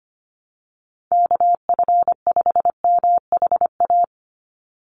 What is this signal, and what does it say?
Morse code, heard as a keyed tone, about 25 wpm, 700 Hz. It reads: KF5MHA
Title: USA Call Signs 25wpm